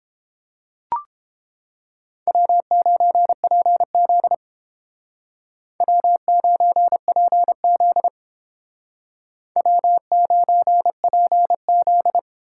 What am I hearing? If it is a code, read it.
Morse code, heard as a keyed tone, about 30 wpm, 700 Hz. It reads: W9PZ W9PZ W9PZ